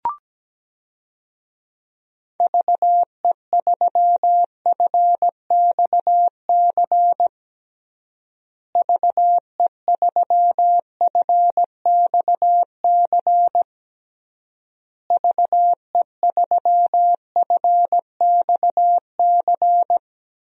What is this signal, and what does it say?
Morse code, heard as a keyed tone, about 17 wpm, 700 Hz. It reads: VE3FXC VE3FXC VE3FXC
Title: DXCC-100 Call Signs - 6 Characters - Repeated 3x - 17wpm